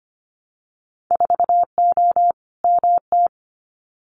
Morse code, heard as a keyed tone, about 25 wpm, 700 Hz. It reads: 4O MT
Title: Winter Field Day Exchanges Encore 25wpm